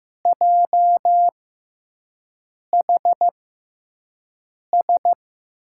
Morse code, heard as a keyed tone, about 15 wpm, spaced as for 5 wpm, 700 Hz. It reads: JHS
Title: USA Call Signs - 6 Characters - 6x Character Spacing - 15wpm